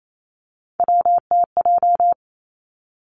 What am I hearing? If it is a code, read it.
Morse code, heard as a keyed tone, about 28 wpm, 700 Hz. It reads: WTJ
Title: Three Challenging Letter-Number Combinations 28wpm